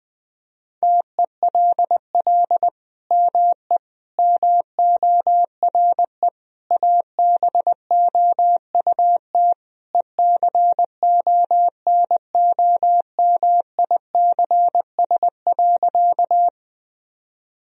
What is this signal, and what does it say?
Morse code, heard as a keyed tone, about 21 wpm, 700 Hz. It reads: TELL ME MORE ABOUT ECONOMICS.
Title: Sentences from Top 1000 Words - Review Part 3 - 20wpm